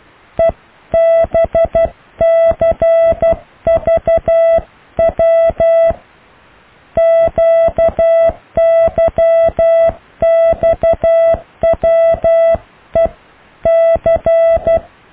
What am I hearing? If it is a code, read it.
Morse code, heard as a keyed tone, about 12 wpm, 660 Hz. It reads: EBCVW QYXWE C
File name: cw / Alle letters